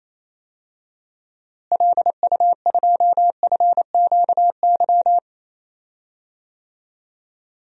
Morse code, heard as a keyed tone, about 27 wpm, 700 Hz. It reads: LU2FQY
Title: DXCC-100 Call Signs - 6 Characters - Repeated 2x - 28wpm